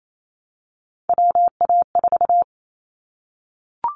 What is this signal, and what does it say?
Morse code, heard as a keyed tone, about 28 wpm, 700 Hz. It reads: WA4
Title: Callsign Prefixes for Top 25 DXCC - Repeated 3x - 28wpm